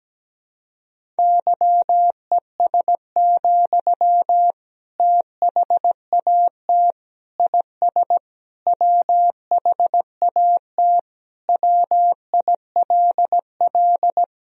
Morse code, heard as a keyed tone, about 17 wpm, 700 Hz. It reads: YES, THAT IS WHAT WILL
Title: Sentences from Top 500 Words and Abbreviations 17wpm